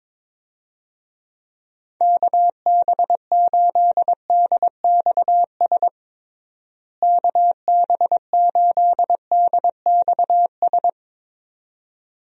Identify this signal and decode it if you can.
Morse code, heard as a keyed tone, about 21 wpm, 700 Hz. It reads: KB8DXS KB8DXS
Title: USA Call Signs - 6 Characters - Repeated 2x - 22wpm